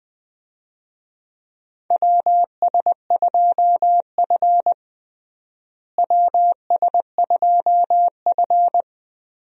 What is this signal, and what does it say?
Morse code, heard as a keyed tone, about 21 wpm, 700 Hz. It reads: WS2F WS2F
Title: USA Call Signs - 4 Characters - Repeated 3x - 20wpm